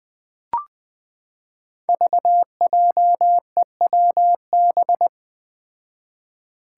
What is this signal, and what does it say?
Morse code, heard as a keyed tone, about 20 wpm, 700 Hz. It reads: VJEWB
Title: Five Challenging Letter-Number Combinations 20wpm